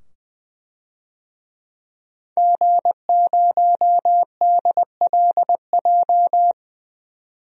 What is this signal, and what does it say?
Morse code, heard as a keyed tone, about 20 wpm, 700 Hz. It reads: G0DLJ
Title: DXCC-100 Call Signs - 5 Characters - 20wpm